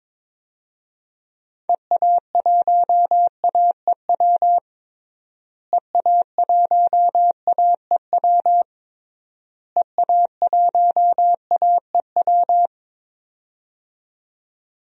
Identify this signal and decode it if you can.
Morse code, heard as a keyed tone, about 22 wpm, 700 Hz. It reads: EA1AEW EA1AEW EA1AEW
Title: DXCC-100 Call Signs - 6 Characters - Repeated 3x - 22wpm